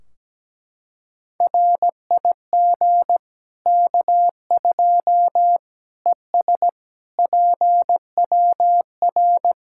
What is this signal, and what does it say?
Morse code, heard as a keyed tone, about 17 wpm, 700 Hz. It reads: RIG K2 ES PWR
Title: Practice QSO - Exchange 2 - Info - 17wpm